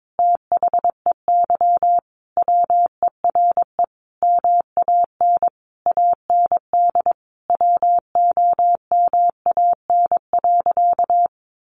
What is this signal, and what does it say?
Morse code, heard as a keyed tone, about 22 wpm, 700 Hz. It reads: THEY WERE MAN AND WOMAN.
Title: Sentences from Top 1100 Words 22wpm